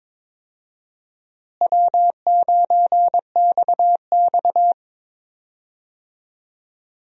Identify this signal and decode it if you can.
Morse code, heard as a keyed tone, about 22 wpm, 700 Hz. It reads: W9XX
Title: USA Call Signs - 4 Characters - Repeated 3x - 22wpm